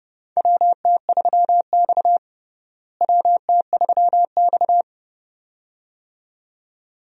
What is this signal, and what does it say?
Morse code, heard as a keyed tone, about 30 wpm, 700 Hz. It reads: WT3X WT3X
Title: USA Call Signs - 4 Characters - Repeated 2x - 30wpm